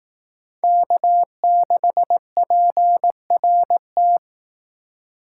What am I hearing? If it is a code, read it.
Morse code, heard as a keyed tone, about 18 wpm, 700 Hz. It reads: K6PRT
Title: Speed Racing - USA 1x3 Call Signs 15wpm